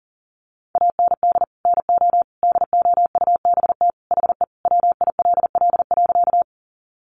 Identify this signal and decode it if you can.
Morse code, heard as a keyed tone, about 37 wpm, 700 Hz. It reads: AND NO DOUBT HE WILL.
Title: Sentences from Top 1500 Words 40wpm